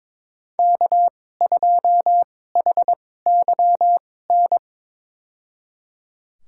Morse code, heard as a keyed tone, about 22 wpm, spaced as for 16 wpm, 700 Hz. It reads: K2HYN
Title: USA Call Signs - 5 Characters - 2x Character Spacing - 22wpm